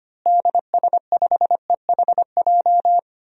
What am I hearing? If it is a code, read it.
Morse code, heard as a keyed tone, about 25 wpm, 700 Hz. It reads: DS5EHJ
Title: DXCC-100 Call Signs - 6 Characters - Repeated 2x - 25wpm